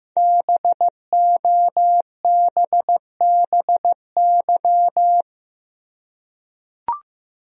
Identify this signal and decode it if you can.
Morse code, heard as a keyed tone, about 15 wpm, 700 Hz. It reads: BOBBY
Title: USA Common Names - 15wpm